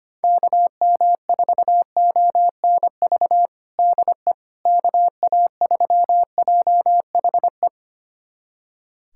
Morse code, heard as a keyed tone, about 25 wpm, 700 Hz. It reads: KM4ONV DE KA3JHE
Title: USA Callsign DE Callsign - 6 Characters - 25wpm